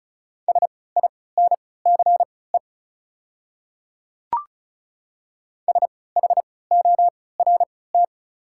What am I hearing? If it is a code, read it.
Morse code, heard as a keyed tone, about 34 wpm, spaced as for 20 wpm, 700 Hz. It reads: SINCE SHORT
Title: 3x Character Spacing - Top 500 Words - 35wpm